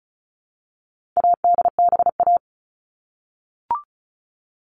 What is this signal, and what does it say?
Morse code, heard as a keyed tone, about 34 wpm, 700 Hz. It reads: ADBA
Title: All Cut Numbers - 4-Digit 35wpm